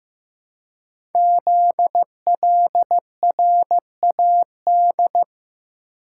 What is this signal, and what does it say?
Morse code, heard as a keyed tone, about 15 wpm, 700 Hz. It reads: ZLRAD
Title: Code Groups - 5 Letter:Numbers - 15wpm